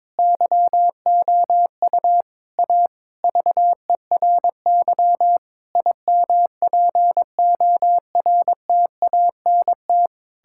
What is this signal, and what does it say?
Morse code, heard as a keyed tone, about 22 wpm, 700 Hz. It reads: YOU A VERY IMPORTANT
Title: Sentences from Top 1000 Words - Review Part 6 - 22wpm